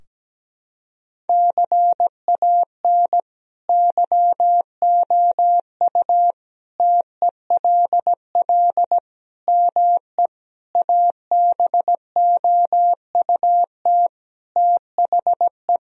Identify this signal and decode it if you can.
Morse code, heard as a keyed tone, about 17 wpm, 700 Hz. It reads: CAN YOU TELL ME ABOUT THE
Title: Sentences from Top 1000 Words - Review Part 3 - 17wpm